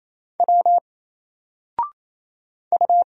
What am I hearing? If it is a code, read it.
Morse code, heard as a keyed tone, about 27 wpm, 700 Hz. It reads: W U
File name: Single-Letter-Number-28wpm